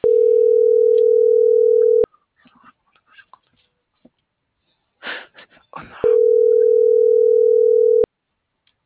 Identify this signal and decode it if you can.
Ringback tone (North America), 440 + 480 Hz.